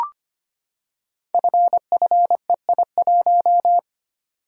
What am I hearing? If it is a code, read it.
Morse code, heard as a keyed tone, about 25 wpm, 700 Hz. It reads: FFEI1
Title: Code Groups - 5 Letter:Numbers - 25wpm